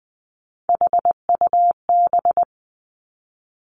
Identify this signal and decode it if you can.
Morse code, heard as a keyed tone, about 20 wpm, 700 Hz. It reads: HUB
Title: Three Challenging Letter-Number Combinations 20wpm